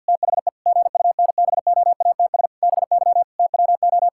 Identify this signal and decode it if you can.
Morse code, heard as a keyed tone, about 50 wpm, 700 Hz. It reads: THE GUNBOATS BY TWO
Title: Pangram Challenge v2 - 50wpm